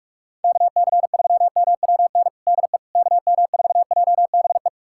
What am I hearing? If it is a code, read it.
Morse code, heard as a keyed tone, about 46 wpm, 700 Hz. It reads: KC3MWN DE KM4JBE
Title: USA Callsign DE Callsign - 6 Characters - 45wpm